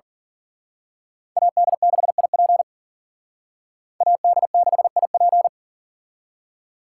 Morse code, heard as a keyed tone, about 42 wpm, 700 Hz. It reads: AD6IP AD6IP
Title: Speed Racing - USA 2x2 Call Signs 40wpm